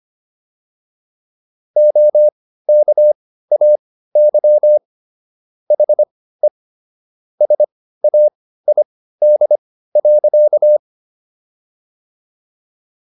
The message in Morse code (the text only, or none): OKAY HE SAID.